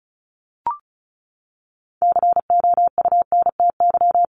Morse code, heard as a keyed tone, about 34 wpm, 700 Hz. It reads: COUNTY
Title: Top 1001-1250 Words 35wpm